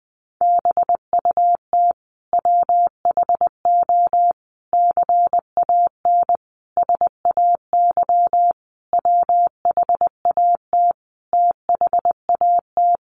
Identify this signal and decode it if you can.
Morse code, heard as a keyed tone, about 20 wpm, 700 Hz. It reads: BUT WHO CAN SAY WHAT THAT
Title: Sentences from Top 1500 Words 20wpm